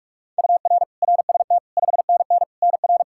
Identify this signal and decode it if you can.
Morse code, heard as a keyed tone, about 46 wpm, 700 Hz. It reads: UR RST 5NN NR
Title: Practice QSO - Exchange 1 - Beginning - 45wpm